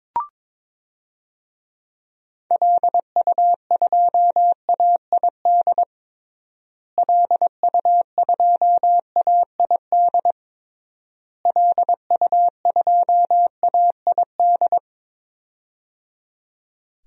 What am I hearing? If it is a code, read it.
Morse code, heard as a keyed tone, about 21 wpm, 700 Hz. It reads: LU2AID LU2AID LU2AID
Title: DXCC-100 Call Signs - 6 Characters - Repeated 3x - 22wpm